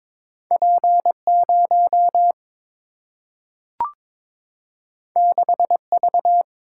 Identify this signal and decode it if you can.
Morse code, heard as a keyed tone, about 22 wpm, 700 Hz. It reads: P 0 6 V
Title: Single-Letter Single-Number 22wpm